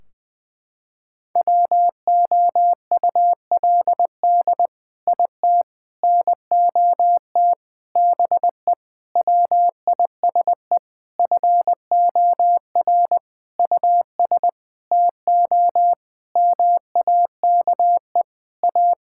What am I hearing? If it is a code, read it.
Morse code, heard as a keyed tone, about 21 wpm, 700 Hz. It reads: WOULD IT NOT BE WISE FOR US TO MAKE A
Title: Sentences from Top 1600 Words 20wpm